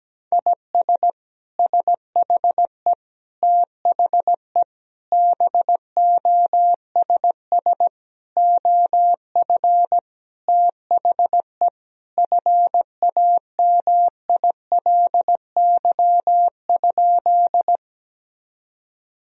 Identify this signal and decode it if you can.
Morse code, heard as a keyed tone, about 17 wpm, 700 Hz. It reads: IS SHE THE BOSS OF THE FAMILY?
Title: Sentences from Top 1000 Words - Review Part 4 - 17wpm